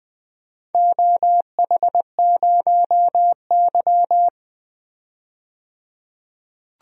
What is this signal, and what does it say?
Morse code, heard as a keyed tone, about 21 wpm, 700 Hz. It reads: OH0Y
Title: DXCC-100 Call Signs - 4 Characters - 20wpm